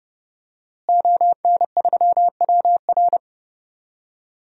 Morse code, heard as a keyed tone, about 31 wpm, 700 Hz. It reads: ON3WR
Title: DXCC-100 Call Signs - 5 Characters - Repeated 3x - 30wpm